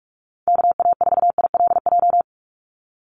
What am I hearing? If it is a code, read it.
Morse code, heard as a keyed tone, about 44 wpm, 700 Hz. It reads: KA4ILJ
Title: USA Call Signs 45wpm